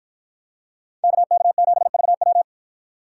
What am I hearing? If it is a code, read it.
Morse code, heard as a keyed tone, about 53 wpm, 700 Hz. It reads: KK7VW